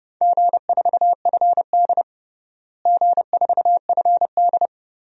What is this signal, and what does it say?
Morse code, heard as a keyed tone, about 30 wpm, 700 Hz. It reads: G4FD G4FD
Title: DXCC-100 Call Signs - 4 Characters - Repeated 3x - 30wpm